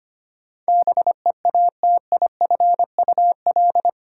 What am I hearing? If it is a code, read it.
Morse code, heard as a keyed tone, about 25 wpm, 700 Hz. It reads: BEATIFUL
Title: Sentences from Top 1300 Words - New Word Drill - 25wpm